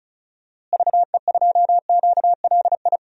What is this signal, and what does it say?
Morse code, heard as a keyed tone, about 34 wpm, 700 Hz. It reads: VE2QLI
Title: DXCC-100 Call Signs - 6 Characters - Repeated 3x - 35wpm